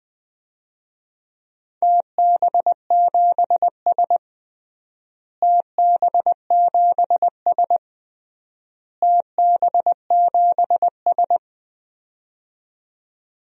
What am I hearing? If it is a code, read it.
Morse code, heard as a keyed tone, about 19 wpm, 700 Hz. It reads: TB7S TB7S TB7S
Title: DXCC-100 Call Signs - 4 Characters - Repeated 3x - 20wpm